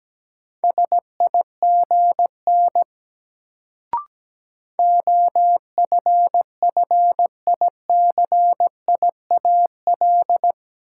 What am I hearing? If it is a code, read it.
Morse code, heard as a keyed tone, about 17 wpm, 700 Hz. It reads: SIGN OFFICIAL
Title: Sentences from Top 1000 Words - New Word Drill - 17wpm